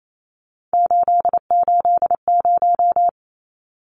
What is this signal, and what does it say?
Morse code, heard as a keyed tone, about 28 wpm, 700 Hz. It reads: 880
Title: Numbers - 3-Digit 28wpm